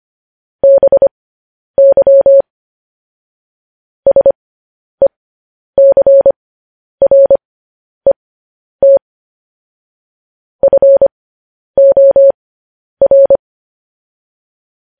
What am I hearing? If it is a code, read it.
Morse code, heard as a keyed tone, about 25 wpm, spaced as for 10 wpm, 550 Hz. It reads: BY SECRET FOR
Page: news headlines in morse code at 10 wpm